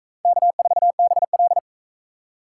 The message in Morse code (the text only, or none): K4BL